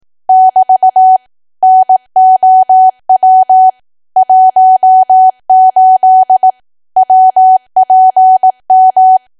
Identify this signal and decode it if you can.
Morse code, heard as a keyed tone, about 18 wpm, 750 Hz. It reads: <BT> NOW 18 WPM
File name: Archive / 18 WPM / 18WPM